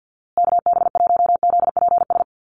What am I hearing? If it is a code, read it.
Morse code, heard as a keyed tone, about 50 wpm, 700 Hz. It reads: KB1ZPS
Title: USA Call Signs Repeated 2x 50wpm